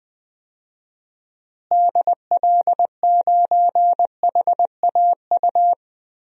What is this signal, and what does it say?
Morse code, heard as a keyed tone, about 19 wpm, 700 Hz. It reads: DL9HAU
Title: DXCC-100 Call Signs - 6 Characters - Repeated 3x - 20wpm